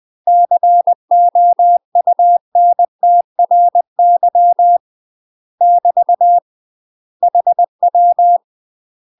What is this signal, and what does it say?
Morse code, heard as a keyed tone, about 20 wpm, 700 Hz. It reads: COUNTRY <BT> HW
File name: UK MASTER QSOs TRAINING / 20WPM / UK MASTER QSOs 0051